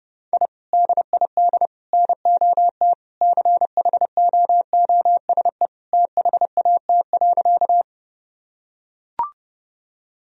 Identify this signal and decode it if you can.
Morse code, heard as a keyed tone, about 30 wpm, 700 Hz. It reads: I DID NOT CHOOSE THAT.
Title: Sentences from Top 1000 Words - Review Part 4 - 30wpm